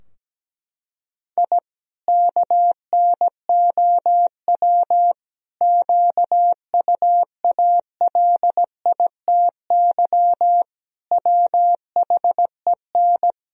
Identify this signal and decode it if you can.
Morse code, heard as a keyed tone, about 17 wpm, 700 Hz. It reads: I KNOW QUALITY WHEN
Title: Sentences from Top 1000 Words - Review Part 4 - 17wpm